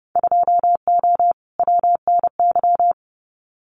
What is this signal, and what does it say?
Morse code, heard as a keyed tone, about 30 wpm, 700 Hz. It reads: 2O WNY
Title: Winter Field Day Exchanges Encore 30wpm